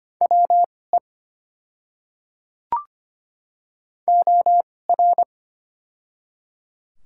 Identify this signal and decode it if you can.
Morse code, heard as a keyed tone, about 25 wpm, spaced as for 18 wpm, 700 Hz. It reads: WE OR
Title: 2x Character Spacing - Top 300 Words - 25wpm